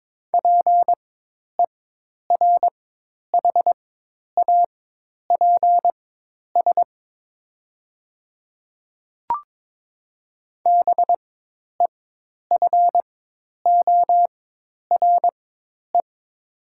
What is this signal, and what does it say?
Morse code, heard as a keyed tone, about 22 wpm, spaced as for 10 wpm, 700 Hz. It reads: PERHAPS BEFORE
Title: 4x Character Spacing - Top 500 Words - 22wpm